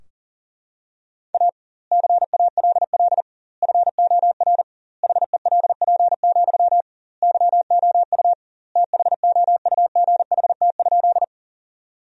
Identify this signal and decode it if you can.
Morse code, heard as a keyed tone, about 40 wpm, 700 Hz. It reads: A CALL FOR HELP, YOU THOUGHT?
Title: 2x Word-Spacing - Sentences from Top 300 Words - 40wpm